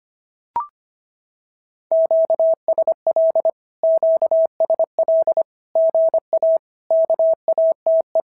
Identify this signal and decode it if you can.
Morse code, heard as a keyed tone, about 25 wpm, 650 Hz. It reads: QSL QSL GA KATE
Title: SOTA QSO Practice 25wpm